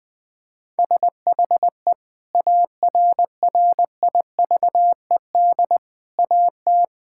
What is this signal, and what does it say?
Morse code, heard as a keyed tone, about 20 wpm, 700 Hz. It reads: SHE ARRIVED AT
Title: Sentences with 5-Word Phrases - Spoken Beginning - 20wpm